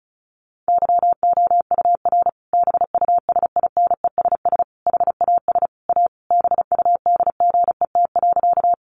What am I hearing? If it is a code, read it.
Morse code, heard as a keyed tone, about 34 wpm, 700 Hz. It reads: YOUR BUSINESS HAS A BUDGET.
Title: Sentences from Top 1200 Words 35wpm